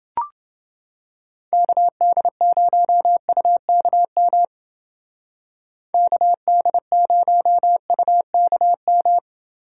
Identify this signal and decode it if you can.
Morse code, heard as a keyed tone, about 29 wpm, 700 Hz. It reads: KD0UKM KD0UKM